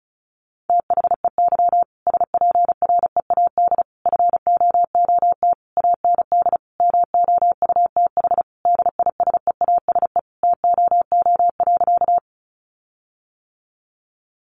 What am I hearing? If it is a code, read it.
Morse code, heard as a keyed tone, about 34 wpm, 700 Hz. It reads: THEY SPREAD FOOT AND MOUTH DISEASE TOO.
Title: Sentences from Top 1500 Words 35wpm